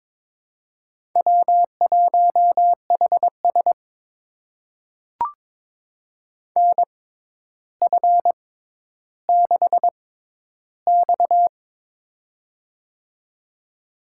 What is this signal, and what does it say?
Morse code, heard as a keyed tone, about 22 wpm, spaced as for 11 wpm, 700 Hz. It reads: W1HS NF6X
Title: USA Call Signs - 4 Characters - 6x Character Spacing - 22wpm